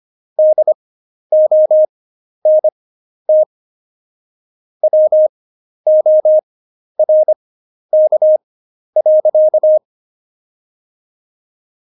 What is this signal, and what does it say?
Morse code, heard as a keyed tone, about 25 wpm, spaced as for 11 wpm, 620 Hz. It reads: DONT WORK.
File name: Jokes-25-char-10-eff